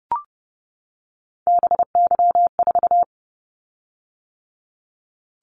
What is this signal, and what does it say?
Morse code, heard as a keyed tone, about 30 wpm, 700 Hz. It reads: BY4
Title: Callsign Prefixes for Top 100 DXCC - 30wpm